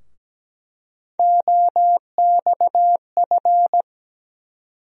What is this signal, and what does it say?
Morse code, heard as a keyed tone, about 17 wpm, 700 Hz. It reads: OXF
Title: USA Call Sign Postfixes - 3 Letters 17wpm